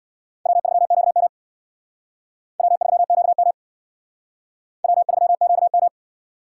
Morse code, heard as a keyed tone, about 61 wpm, 700 Hz. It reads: W3QM W3QM W3QM
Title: Speed Racing - USA Call Signs 50wpm